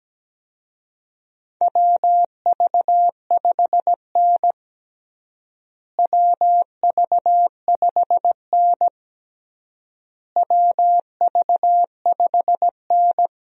Morse code, heard as a keyed tone, about 17 wpm, 700 Hz. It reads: WV5N WV5N WV5N
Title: USA Call Signs - 4 Characters - Repeated 3x - 17wpm